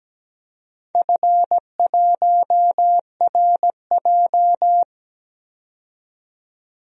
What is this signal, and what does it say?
Morse code, heard as a keyed tone, about 17 wpm, 700 Hz. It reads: F1RJ
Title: DXCC-100 Call Signs - 4 Characters - 17wpm